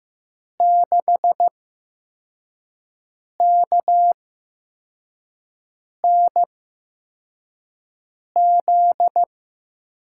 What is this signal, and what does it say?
Morse code, heard as a keyed tone, about 15 wpm, spaced as for 4 wpm, 700 Hz. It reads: 6KNZ
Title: USA Call Signs - 6 Characters - 8x Character Spacing - 15wpm